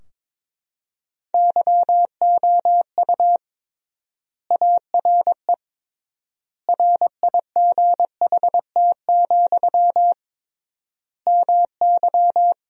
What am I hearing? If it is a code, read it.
Morse code, heard as a keyed tone, about 21 wpm, 700 Hz. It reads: YOU ARE RIGHT, MY
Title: 3x Word-Spacing - Sentences from Top 400 Words - 22wpm